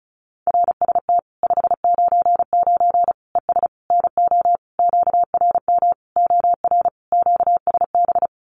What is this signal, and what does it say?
Morse code, heard as a keyed tone, about 34 wpm, 700 Hz. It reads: RST 599 ES NO QRM OR QSB
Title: Common QSO Phrases 35wpm